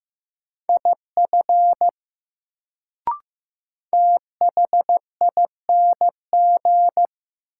Morse code, heard as a keyed tone, about 15 wpm, 700 Hz. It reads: IF THING
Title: Sentences from Top 100 Words - New Word Drill - 15wpm